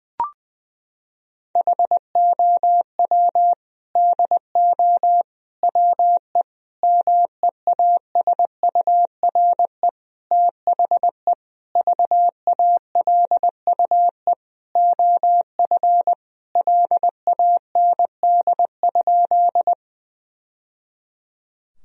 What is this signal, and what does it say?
Morse code, heard as a keyed tone, about 20 wpm, 700 Hz. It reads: HOW DO WE MEASURE THE VALUE OF LAND?
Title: Sentences from Top 1000 Words - Review Part 3 - 20wpm